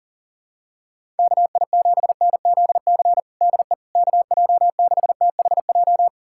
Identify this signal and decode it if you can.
Morse code, heard as a keyed tone, about 40 wpm, 700 Hz. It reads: KI7NZC DE KJ6THJ
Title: USA Callsign DE Callsign - 6 Characters - 40wpm